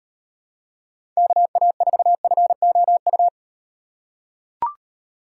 Morse code, heard as a keyed tone, about 36 wpm, 700 Hz. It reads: KA4FOU